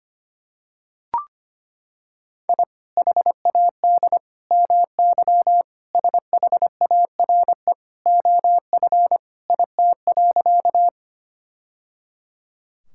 Morse code, heard as a keyed tone, about 25 wpm, 700 Hz. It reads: I HAD MY SHARE OF IT.
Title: Sentences from Top 1000 Words - Review Part 6 - 25wpm